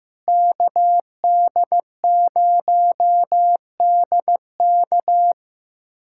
Morse code, heard as a keyed tone, about 15 wpm, 700 Hz. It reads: KD0DK
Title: USA Callsign DE Callsign - 5 Characters - 15wpm